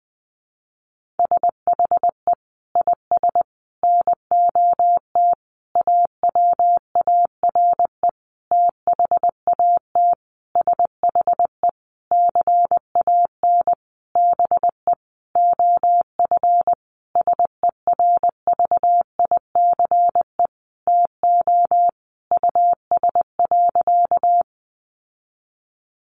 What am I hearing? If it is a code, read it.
Morse code, heard as a keyed tone, about 19 wpm, 700 Hz. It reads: SHE IS NOT AWARE THAT SHE CAN BE OF SERVICE TO US.
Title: Sentences from Top 1200 Words 20wpm